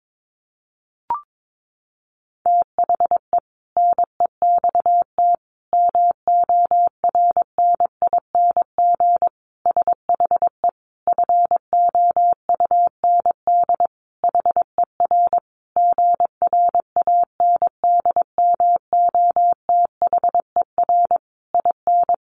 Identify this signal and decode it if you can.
Morse code, heard as a keyed tone, about 22 wpm, 700 Hz. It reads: THE NEXT MORNING SHE FOUND HER GRANDMOTHER IN
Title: Sentences from Top 1200 Words 22wpm